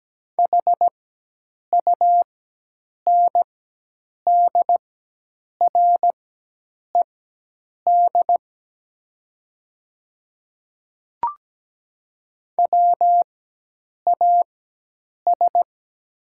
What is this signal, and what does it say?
Morse code, heard as a keyed tone, about 17 wpm, spaced as for 8 wpm, 700 Hz. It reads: HUNDRED WAS
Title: 4x Character Spacing - Top 500 Words - 17wpm